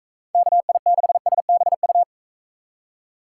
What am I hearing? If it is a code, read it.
Morse code, heard as a keyed tone, about 45 wpm, 700 Hz. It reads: KI6SBU